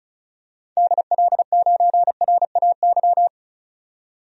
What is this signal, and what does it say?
Morse code, heard as a keyed tone, about 34 wpm, 700 Hz. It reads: DL9RAY
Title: DXCC-100 Call Signs - 6 Characters - 35wpm